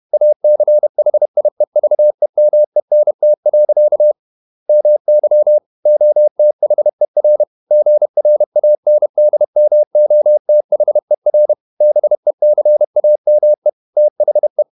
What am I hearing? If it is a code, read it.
Morse code, heard as a keyed tone, about 31 wpm, 600 Hz. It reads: ACHIEVEMENT. MY OTHER GRANDMOTHER BECAME THE
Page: Never 31wpm | CW med Gnister